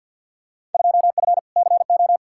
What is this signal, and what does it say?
Morse code, heard as a keyed tone, about 52 wpm, 700 Hz. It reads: 2F CO
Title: Field Day Exchanges - 50wpm